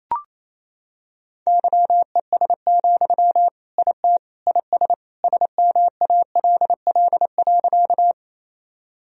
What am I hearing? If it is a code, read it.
Morse code, heard as a keyed tone, about 28 wpm, 700 Hz. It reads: YES, IT IS SMALL.
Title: Sentences from Top 1000 Words - Review Part 1 - 28wpm